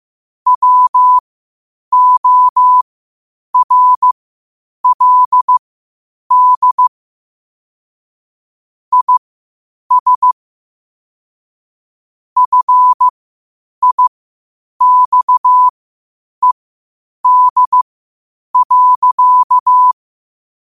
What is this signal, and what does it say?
Morse code, heard as a keyed tone, about 15 wpm, spaced as for 8 wpm, 1000 Hz. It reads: WORLD IS FIXED.
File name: qotdfiles / 8wpm